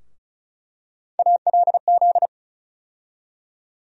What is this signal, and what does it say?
Morse code, heard as a keyed tone, about 34 wpm, 700 Hz. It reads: ALZ
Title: USA Call Sign Postfixes - 3 Letters 35wpm